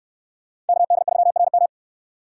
Morse code, heard as a keyed tone, about 61 wpm, 700 Hz. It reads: KD3RM